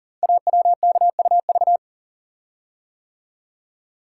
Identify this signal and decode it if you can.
Morse code, heard as a keyed tone, about 42 wpm, 690 Hz. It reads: AWKUV
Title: Code Groups - 5 Letters - 40wpm